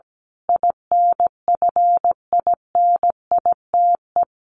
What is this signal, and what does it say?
Morse code, heard as a keyed tone, about 17 wpm, 700 Hz. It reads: INFINITE
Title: A Sound Of Thunder 17wpm